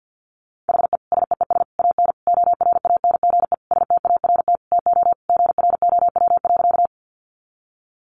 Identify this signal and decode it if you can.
Morse code, heard as a keyed tone, about 50 wpm, 700 Hz. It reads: HE SEES AN ORANGE START TO GROW.
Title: Easy Phrases 50wpm